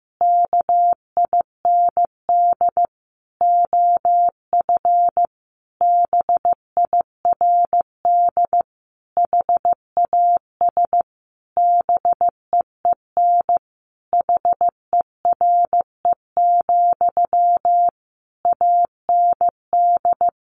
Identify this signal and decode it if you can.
Morse code, heard as a keyed tone, about 15 wpm, 700 Hz. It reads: KIND OF BIRD HAS BEEN HERE, AND
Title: Sentences from Top 1200 Words 15wpm